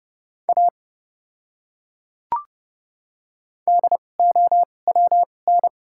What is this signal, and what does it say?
Morse code, heard as a keyed tone, about 30 wpm, spaced as for 22 wpm, 700 Hz. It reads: A DOWN
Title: 2x Character Spacing - Top 200 Words - 30wpm